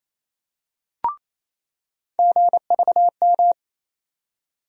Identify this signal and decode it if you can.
Morse code, heard as a keyed tone, about 28 wpm, 700 Hz. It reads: GVM
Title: USA Call Sign Postfixes - 3 Letters 28wpm